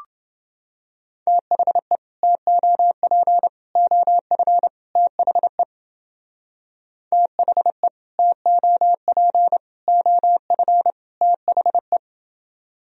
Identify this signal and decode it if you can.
Morse code, heard as a keyed tone, about 31 wpm, 700 Hz. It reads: THE TOP OF THE THE TOP OF THE